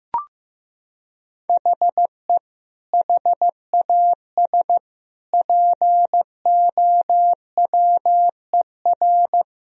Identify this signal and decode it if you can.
Morse code, heard as a keyed tone, about 15 wpm, 700 Hz. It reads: HE HAS POWER
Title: Sentences from Top 1000 Words - Review Part 2 - 15wpm